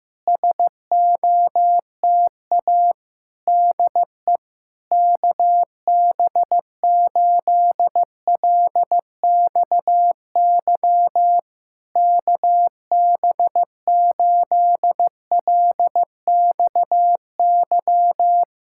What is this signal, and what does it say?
Morse code, heard as a keyed tone, about 15 wpm, 700 Hz. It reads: SOTA DE KB8LXY KB8LXY
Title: Speed-Racing Common QSO Phrases Encore - 15wpm